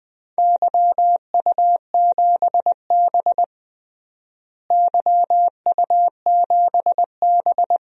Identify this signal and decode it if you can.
Morse code, heard as a keyed tone, about 21 wpm, 700 Hz. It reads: YU7B YU7B
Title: DXCC-100 Call Signs - 4 Characters - Repeated 3x - 20wpm